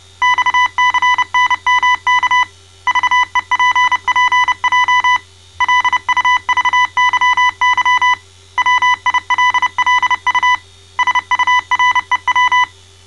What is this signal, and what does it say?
Morse code, heard as a keyed tone, about 29 wpm, 1000 Hz. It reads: XCNMK VEPPJ LUVYY WILLU SUREW
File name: mrx / malta